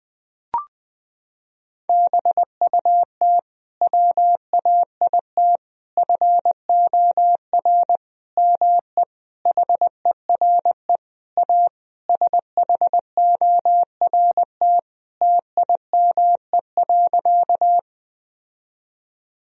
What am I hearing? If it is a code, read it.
Morse code, heard as a keyed tone, about 20 wpm, 700 Hz. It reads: BUT WAIT FOR ME HERE A SHORT TIME.
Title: Sentences from Top 1000 Words - Review Part 2 - 20wpm